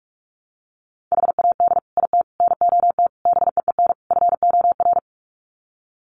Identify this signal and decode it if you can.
Morse code, heard as a keyed tone, about 44 wpm, 700 Hz. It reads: HAD IT NOT BEEN FOR
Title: Sets of 5 Words Encore2 45wpm